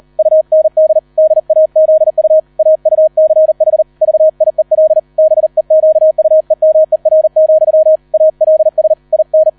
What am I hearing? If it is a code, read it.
Morse code, heard as a keyed tone, about 38 wpm, 630 Hz. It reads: UND DAZU AUCH VIEL BEQUEMER, ALS IN